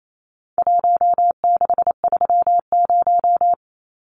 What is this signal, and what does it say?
Morse code, heard as a keyed tone, about 28 wpm, 700 Hz. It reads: 1630
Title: Numbers - Contest Style - 4-Digit 28wpm